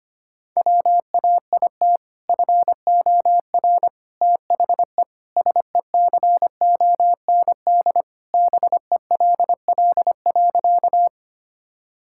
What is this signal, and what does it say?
Morse code, heard as a keyed tone, about 25 wpm, 700 Hz. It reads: WAIT FOR THE SECOND BELL.
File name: practice / 25wpm / Sentences with 2-Word Phrases - 25wpm